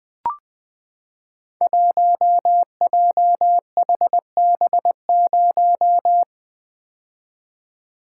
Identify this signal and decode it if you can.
Morse code, heard as a keyed tone, about 20 wpm, 700 Hz. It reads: 1JHB0
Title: Code Groups - 5 Letter:Numbers - 20wpm